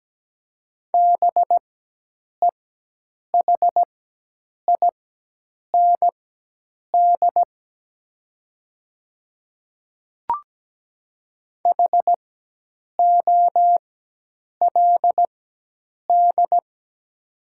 Morse code, heard as a keyed tone, about 17 wpm, spaced as for 8 wpm, 700 Hz. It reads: BEHIND HOLD
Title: 4x Character Spacing - Top 500 Words - 17wpm